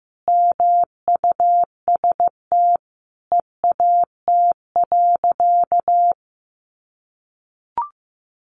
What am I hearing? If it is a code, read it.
Morse code, heard as a keyed tone, about 15 wpm, 700 Hz. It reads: MUST EAT.
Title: Sentences from Top 300 Words 15wpm